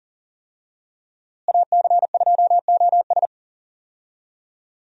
Morse code, heard as a keyed tone, about 40 wpm, 700 Hz. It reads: AC2OS
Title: USA Call Signs - 5 Characters - Repeated 2x - 40wpm